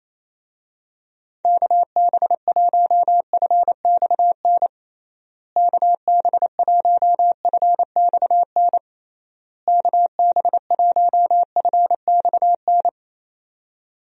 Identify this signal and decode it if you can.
Morse code, heard as a keyed tone, about 28 wpm, 700 Hz. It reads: KB1FXN KB1FXN KB1FXN
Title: USA Call Signs - 6 Characters - Repeated 3x - 28wpm